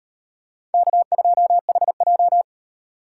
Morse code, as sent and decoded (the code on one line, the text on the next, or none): -.- ..--- .... .---
K2HJ